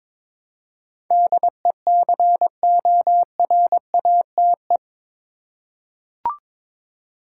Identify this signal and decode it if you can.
Morse code, heard as a keyed tone, about 22 wpm, 700 Hz. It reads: DECORATE
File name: practice / 22wpm / Eight Letter Words - 22wpm